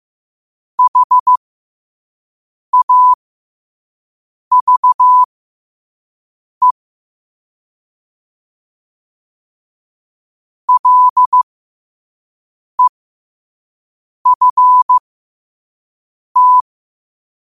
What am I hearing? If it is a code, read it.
Morse code, heard as a keyed tone, about 15 wpm, spaced as for 5 wpm, 1000 Hz. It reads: HAVE LEFT